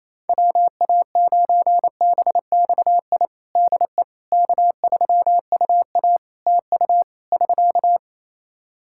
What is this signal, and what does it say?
Morse code, heard as a keyed tone, about 28 wpm, 700 Hz. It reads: WA9BXI DE K3UA TU <SK>
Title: K1USN SST QSO Practice 28wpm